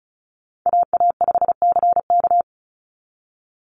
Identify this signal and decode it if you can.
Morse code, heard as a keyed tone, about 34 wpm, 700 Hz. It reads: AA5CK
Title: NA SOTA Chaser Callsigns 35wpm